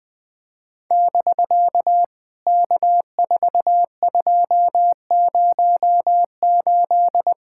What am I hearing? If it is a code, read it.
Morse code, heard as a keyed tone, about 20 wpm, 700 Hz. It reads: <BK> K4208
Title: POTA QSO Practice - 20wpm